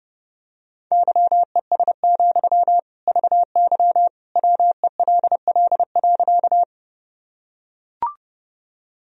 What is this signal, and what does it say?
Morse code, heard as a keyed tone, about 30 wpm, 700 Hz. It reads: YES, VY WELL.
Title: Sentences from Top 500 Words and Abbreviations 30wpm